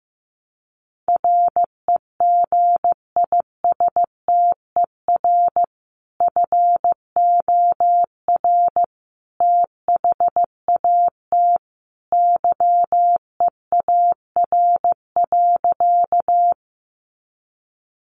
Morse code, heard as a keyed tone, about 15 wpm, 700 Hz. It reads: REGISTER FOR THAT YEAR.
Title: Sentences from Top 1200 Words 15wpm